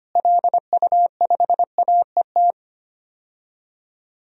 Morse code, heard as a keyed tone, about 25 wpm, 700 Hz. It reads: LU5AET
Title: DXCC-100 Call Signs - 6 Characters - Repeated 2x - 25wpm